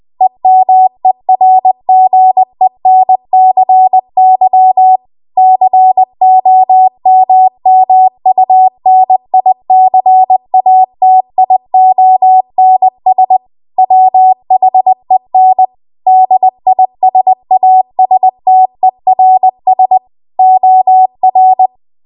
Morse code, heard as a keyed tone, about 20 wpm, 750 Hz. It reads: EMERGENCY COMMUNICATIONS WHEN DISASTERS OR
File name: Morse / 20WPM